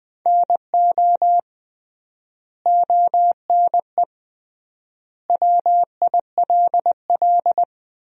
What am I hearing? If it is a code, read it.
Morse code, heard as a keyed tone, about 21 wpm, 700 Hz. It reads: NO ONE WILL
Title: 3x Word-Spacing - Sentences from Top 100 Words - 20wpm